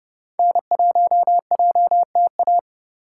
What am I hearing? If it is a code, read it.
Morse code, heard as a keyed tone, about 30 wpm, 700 Hz. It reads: N1JTA
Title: USA Call Signs - 5 Characters - 4x Character Spacing - 30wpm